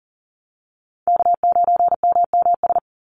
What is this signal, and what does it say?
Morse code, heard as a keyed tone, about 40 wpm, 700 Hz. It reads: K9MMS
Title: Call Signs Repeated 2x - 40wpm